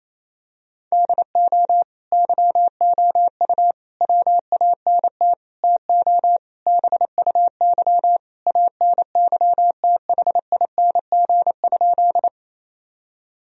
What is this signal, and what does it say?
Morse code, heard as a keyed tone, about 28 wpm, 700 Hz. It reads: DO YOU WANT TO BUY ANYTHING?
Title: Sentences from Top 1000 Words - Review Part 4 - 28wpm